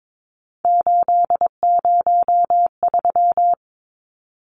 Morse code, heard as a keyed tone, about 22 wpm, 700 Hz. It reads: 803
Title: Numbers - 3-Digit 22wpm